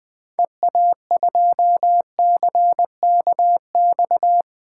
Morse code, heard as a keyed tone, about 20 wpm, 700 Hz. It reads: EA2CKX
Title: 2023 SOTA Chaser Call Signs - 20wpm